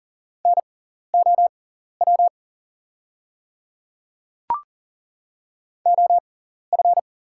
Morse code, heard as a keyed tone, about 40 wpm, spaced as for 14 wpm, 700 Hz. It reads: NOW OF
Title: 6x Character Spacing - Top 300 Words - 40wpm